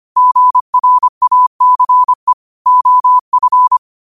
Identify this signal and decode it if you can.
Morse code, heard as a keyed tone, about 25 wpm, 1000 Hz. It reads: GRACE OF
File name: qotdfiles / 25wpm